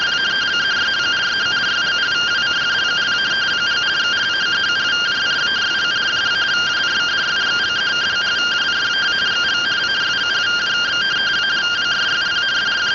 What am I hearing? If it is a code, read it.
Two-tone teleprinter signal, about 170 Hz shift.